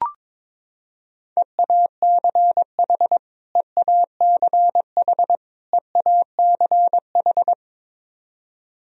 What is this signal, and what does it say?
Morse code, heard as a keyed tone, about 22 wpm, 700 Hz. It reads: EACH EACH EACH
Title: Top 100 Words Repeated 22wpm